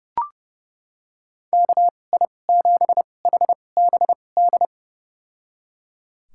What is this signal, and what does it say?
Morse code, heard as a keyed tone, about 30 wpm, spaced as for 22 wpm, 700 Hz. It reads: KI7HBD
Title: USA Call Signs - 6 Characters - 2x Character Spacing - 30wpm